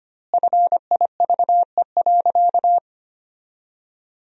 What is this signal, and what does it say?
Morse code, heard as a keyed tone, about 25 wpm, 700 Hz. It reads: FIVE.
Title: 2x Word-Spacing - Sentences from Top 400 Words - 25wpm